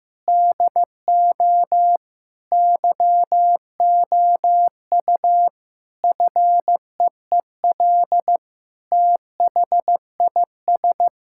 Morse code, heard as a keyed tone, about 16 wpm, 700 Hz. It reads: DO YOU FEEL THIS
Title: Sentences from Top 1600 Words 15wpm